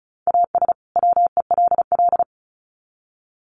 Morse code, heard as a keyed tone, about 34 wpm, 700 Hz. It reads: AS WELL
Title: Sets of 2 Words 35wpm